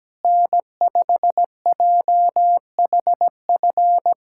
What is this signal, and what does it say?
Morse code, heard as a keyed tone, about 17 wpm, 700 Hz. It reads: N5JHF
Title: USA Callsign DE Callsign - 5 Characters - 17wpm